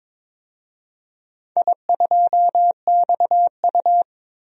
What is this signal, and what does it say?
Morse code, heard as a keyed tone, about 21 wpm, 700 Hz. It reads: I2XU
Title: DXCC-100 Call Signs - 4 Characters - 22wpm